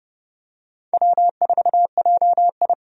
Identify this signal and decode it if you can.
Morse code, heard as a keyed tone, about 30 wpm, 700 Hz. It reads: W4JI
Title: USA Call Signs - 4 Characters - 30wpm